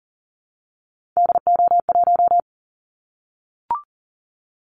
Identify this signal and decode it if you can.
Morse code, heard as a keyed tone, about 40 wpm, 700 Hz. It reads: DO1
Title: Callsign Prefixes for Top 25 DXCC - 40wpm